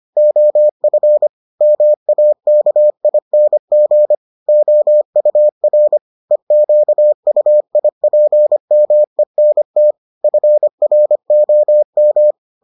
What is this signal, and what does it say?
Morse code, heard as a keyed tone, about 25 wpm, 600 Hz. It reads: OF MAKING OUR EQUIPMENT FROM